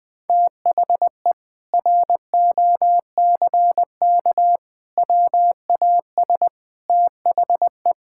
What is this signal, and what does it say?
Morse code, heard as a keyed tone, about 20 wpm, 700 Hz. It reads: THE ROCK WAS THE
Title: Sentences with 4-Word Phrases - 20wpm